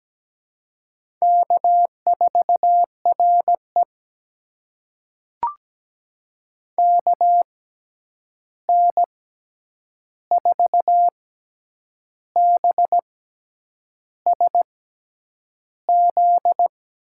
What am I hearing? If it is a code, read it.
Morse code, heard as a keyed tone, about 17 wpm, spaced as for 8 wpm, 700 Hz. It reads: K4RE KN4BSZ
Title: USA Call Signs - 6x Character Spacing - 17wpm